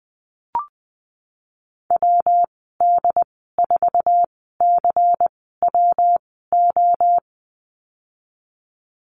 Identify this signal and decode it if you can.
Morse code, heard as a keyed tone, about 19 wpm, spaced as for 14 wpm, 700 Hz. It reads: WD4CWO
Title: 122 Lesson 20g - Number 3 - Call Signs 20wpm